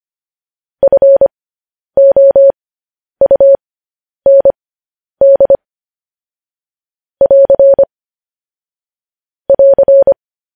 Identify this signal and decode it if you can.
Morse code, heard as a keyed tone, about 25 wpm, spaced as for 10 wpm, 550 Hz. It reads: FOUND <AR> <AR>
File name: podcast / News headlines-1x-25x10